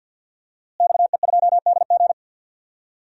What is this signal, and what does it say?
Morse code, heard as a keyed tone, about 45 wpm, 700 Hz. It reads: XE2DG
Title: DXCC-100 Call Signs - 5 Characters - Repeated 2x - 50wpm